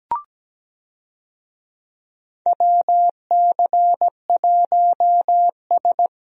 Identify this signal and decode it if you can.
Morse code, heard as a keyed tone, about 17 wpm, 700 Hz. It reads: WC1S
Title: USA Call Signs - 4 Characters - Repeated 3x - 17wpm